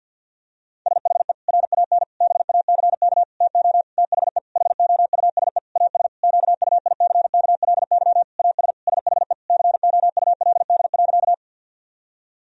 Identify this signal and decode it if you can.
Morse code, heard as a keyed tone, about 50 wpm, 700 Hz. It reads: SHE RAN BACK TO THE HOUSE AS QUICKLY AS SHE COULD.
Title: Sentences from Top 1000 Words - Review Part 3 - 50wpm